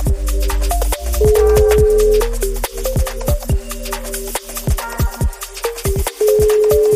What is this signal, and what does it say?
Ringback tone (Europe, 425 Hz).